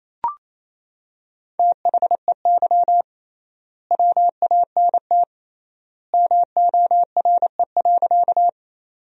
Morse code, heard as a keyed tone, about 28 wpm, 700 Hz. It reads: THEY WANT MORE.
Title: 3x Word-Spacing - Sentences from Top 200 Words - 28wpm